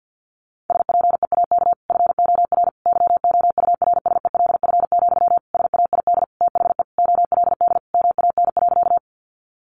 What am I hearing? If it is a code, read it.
Morse code, heard as a keyed tone, about 49 wpm, 700 Hz. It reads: SPEAK FOR YOURSELF, SAID THE OLD MAN.
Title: Sentences from Top 1100 Words 50wpm